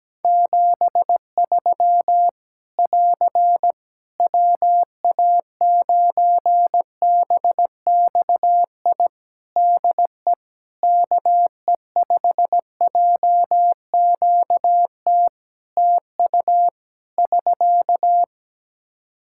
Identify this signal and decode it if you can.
Morse code, heard as a keyed tone, about 17 wpm, 700 Hz. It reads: 73 <AR> WA9BXI DE KE5JQT TU <SK>
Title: Practice QSO - Exchange 4 - Ending - 17wpm